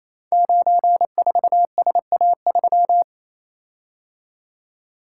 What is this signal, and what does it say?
Morse code, heard as a keyed tone, about 28 wpm, 700 Hz. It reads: 94SA3
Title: Code Groups - 5 Letter:Numbers - 28wpm